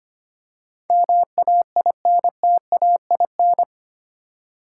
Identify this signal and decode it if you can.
Morse code, heard as a keyed tone, about 25 wpm, 700 Hz. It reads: MAINTAIN
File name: Eight Letter Words - 25wpm